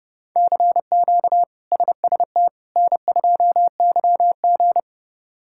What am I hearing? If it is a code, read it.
Morse code, heard as a keyed tone, about 30 wpm, 700 Hz. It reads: CQ SST N2YG
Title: K1USN SST QSO Practice - 30wpm